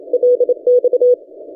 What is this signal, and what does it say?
Morse code, heard as a keyed tone, about 27 wpm, 490 Hz. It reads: LX
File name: ft5zm-17cw-2014-01-27-0120z-lp